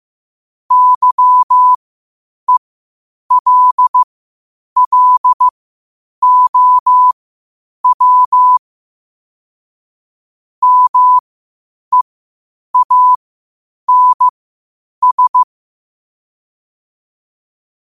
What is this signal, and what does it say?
Morse code, heard as a keyed tone, about 15 wpm, spaced as for 8 wpm, 1000 Hz. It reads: YELLOW MEANS
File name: qotdfiles / 8wpm